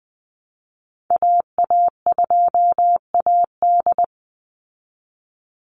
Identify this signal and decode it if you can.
Morse code, heard as a keyed tone, about 19 wpm, 700 Hz. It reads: AA2AD
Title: Call Signs Repeated 2x - 20wpm